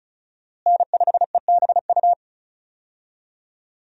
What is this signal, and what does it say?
Morse code, heard as a keyed tone, about 36 wpm, 700 Hz. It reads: N5EBU
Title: USA Call Signs - 5 Characters - Repeated 3x - 35wpm